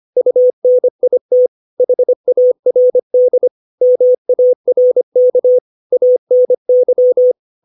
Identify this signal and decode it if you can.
Morse code, heard as a keyed tone, about 25 wpm, 500 Hz. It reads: UNIT HARD MARK ANY
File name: 632 words / 632 Words 25WPM